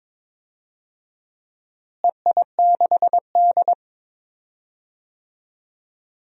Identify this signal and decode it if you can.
Morse code, heard as a keyed tone, about 22 wpm, 700 Hz. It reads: EI6D
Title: DXCC-100 Call Signs - 4 Characters - Repeated 2x - 22wpm